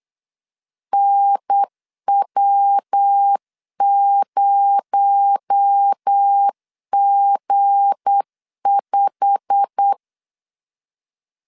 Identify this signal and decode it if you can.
Morse code, heard as a keyed tone, about 8 wpm, 800 Hz. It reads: NW0G5
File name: LEZIONI / 8 YJ45 / 8 YJ45 8wpm 3214